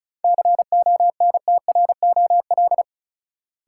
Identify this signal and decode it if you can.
Morse code, heard as a keyed tone, about 36 wpm, 700 Hz. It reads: CONTROL
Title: Sentences from Top 1000 Words - Review Part 2 - New Word Drill - 35wpm